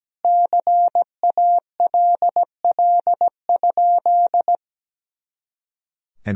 Morse code, heard as a keyed tone, about 17 wpm, 700 Hz. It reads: CALL?
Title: Sentences from Top 100 Words and Abbreviations 17wpm